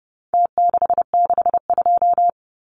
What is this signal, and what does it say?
Morse code, heard as a keyed tone, about 30 wpm, 700 Hz. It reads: T662
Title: Numbers - Contest Style - 4-Digit 30wpm